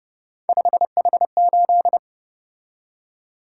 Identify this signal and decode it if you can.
Morse code, heard as a keyed tone, about 29 wpm, 700 Hz. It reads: 5H8
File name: Speed-Racing - Three Letter-Number Combinations - 25wpm